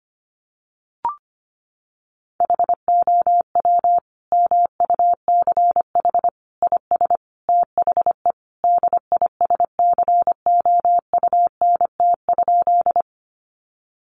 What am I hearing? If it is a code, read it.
Morse code, heard as a keyed tone, about 25 wpm, 700 Hz. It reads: HOW MUCH IS THE DISCOUNT?
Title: Sentences from Top 1200 Words 25wpm